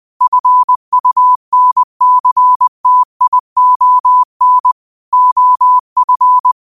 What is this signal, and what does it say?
Morse code, heard as a keyed tone, about 20 wpm, 1000 Hz. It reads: FUNCTION OF
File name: qotdfiles / 20wpm